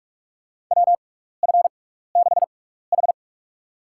Morse code, heard as a keyed tone, about 45 wpm, spaced as for 15 wpm, 700 Hz. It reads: WF6H
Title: USA Call Signs - 4 Characters - 6x Character Spacing - 45wpm